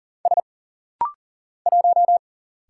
Morse code, heard as a keyed tone, about 40 wpm, 700 Hz. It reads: S 1
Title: Single Letter-Number Alt - Rapid-Fire 40wpm